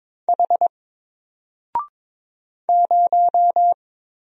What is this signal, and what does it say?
Morse code, heard as a keyed tone, about 22 wpm, 700 Hz. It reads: H 0
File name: Single-Letter-Number-22wpm